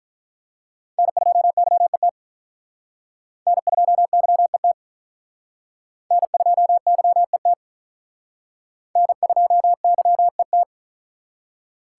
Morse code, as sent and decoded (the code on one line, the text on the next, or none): -. ..--- -.-- . - / -. ..--- -.-- . - / -. ..--- -.-- . - / -. ..--- -.-- . -
N2YET N2YET N2YET N2YET